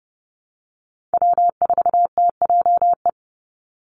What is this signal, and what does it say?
Morse code, heard as a keyed tone, about 30 wpm, 700 Hz. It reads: W4TJE
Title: Call Signs Repeated 2x - 30wpm